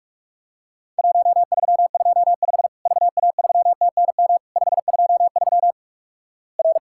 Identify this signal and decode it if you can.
Morse code, heard as a keyed tone, about 48 wpm, 700 Hz. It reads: 1325 VA3TNM 523 R
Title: Worked All Europe DX Contest - QTC Exchange - 45wpm